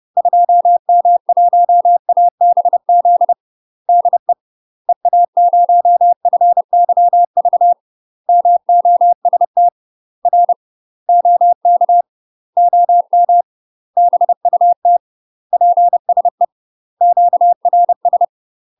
Morse code, heard as a keyed tone, about 30 wpm, 700 Hz. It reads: 2M1ABZ DE EA0FYV MOST R OK OM BUT PSE QRS
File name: UK MASTER QSOs TRAINING / 30WPM / UK MASTER QSOs 0045